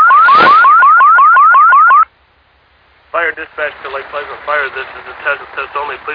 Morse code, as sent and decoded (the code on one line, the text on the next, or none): none